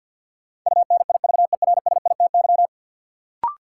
Morse code, heard as a keyed tone, about 53 wpm, 700 Hz. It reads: UNIVERSITY
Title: Sentences from Top 1300 Words - New Word Drill Alt - 50wpm